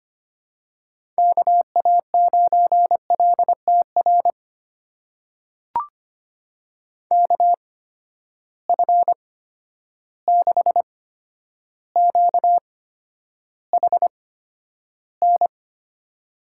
Morse code, heard as a keyed tone, about 25 wpm, spaced as for 11 wpm, 700 Hz. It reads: KA9LTR KF6QHN
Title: USA Call Signs - 6 Characters - 8x Character Spacing - 25wpm